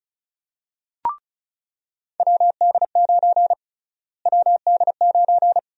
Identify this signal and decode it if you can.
Morse code, heard as a keyed tone, about 34 wpm, 700 Hz. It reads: WD9 WD9
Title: USA Call Sign Prefixes - Repeated 2x - 35wpm